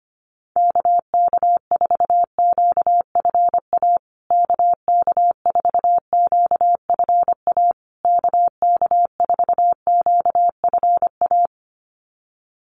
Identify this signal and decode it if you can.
Morse code, heard as a keyed tone, about 25 wpm, 700 Hz. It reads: KK4QFA KK4QFA KK4QFA
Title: USA Call Signs Repeated 3x 25wpm